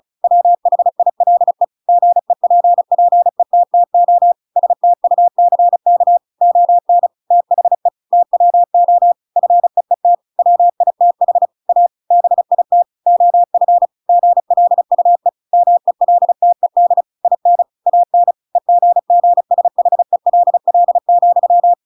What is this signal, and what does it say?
Morse code, heard as a keyed tone, about 34 wpm, 700 Hz. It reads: WHILE GEPPETTO STUCK ON THE TWO FEET WITH A BIT OF GLUE MELTED IN AN EGGSHELL,
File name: Pinocchio / Pinocchio ch8 35 35 ik1hge